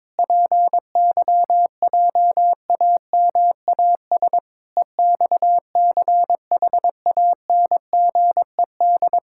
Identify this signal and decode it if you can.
Morse code, heard as a keyed tone, about 22 wpm, 700 Hz. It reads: PYJAMAS EXCHANGED
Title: Pangram Challenge v2 - 22wpm